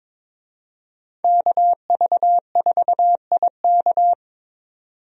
Morse code, heard as a keyed tone, about 22 wpm, 700 Hz. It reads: KV4IK
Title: USA Call Signs - 5 Characters - 22wpm